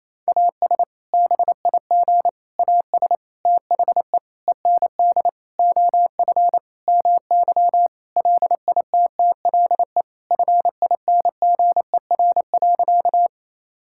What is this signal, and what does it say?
Morse code, heard as a keyed tone, about 28 wpm, 700 Hz. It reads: AS BIG AS THE END OF MY LITTLE FINGER.
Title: Sentences from Top 1600 Words 28wpm